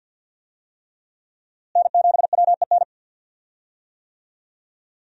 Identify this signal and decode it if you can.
Morse code, heard as a keyed tone, about 52 wpm, 700 Hz. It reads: N7WEN
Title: USA Call Signs - 5 Characters - Repeated 3x - 50wpm